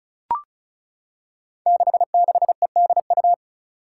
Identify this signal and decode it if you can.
Morse code, heard as a keyed tone, about 36 wpm, 700 Hz. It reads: 66EDU
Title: Five Challenging Letter-Number Combinations 35wpm